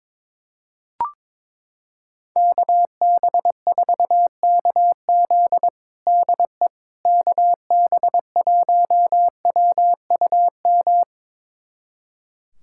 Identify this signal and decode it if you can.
Morse code, heard as a keyed tone, about 21 wpm, 700 Hz. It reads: KB4KZ DE KB1WUM
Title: USA Callsign DE Callsign 22wpm